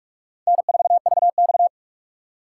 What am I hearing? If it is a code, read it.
Morse code, heard as a keyed tone, about 59 wpm, spaced as for 51 wpm, 700 Hz. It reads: N4VX
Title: Speed Racing - USA Call Signs 45wpm